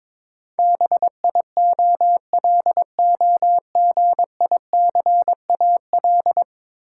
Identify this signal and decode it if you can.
Morse code, heard as a keyed tone, about 22 wpm, 700 Hz. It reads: BIOLOGICAL
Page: morse-code-ninja-practice-22wpm